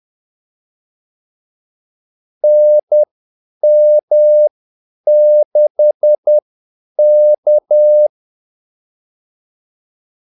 Morse code, heard as a keyed tone, about 10 wpm, spaced as for 8 wpm, 600 Hz. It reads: NM6K
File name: Callsigns-10-8wpm